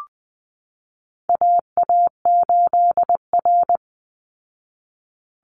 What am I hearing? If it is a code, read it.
Morse code, heard as a keyed tone, about 20 wpm, 700 Hz. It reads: AA8R
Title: Call Signs Repeated 3x - 20wpm